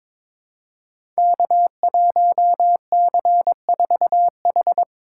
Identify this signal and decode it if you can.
Morse code, heard as a keyed tone, about 23 wpm, 700 Hz. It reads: K1C4H
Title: Code Groups - 5 Letter:Numbers - 22wpm